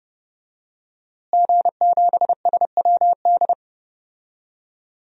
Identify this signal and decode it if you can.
Morse code, heard as a keyed tone, about 32 wpm, 700 Hz. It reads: G7SWD
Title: DXCC-100 Call Signs - 5 Characters - Repeated 2x - 30wpm